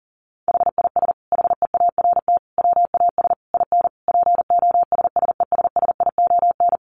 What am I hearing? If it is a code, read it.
Morse code, heard as a keyed tone, about 40 wpm, 700 Hz. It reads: HIS HEART WAS IN POSSESSION
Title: Sentences from Top 1400 Words 40wpm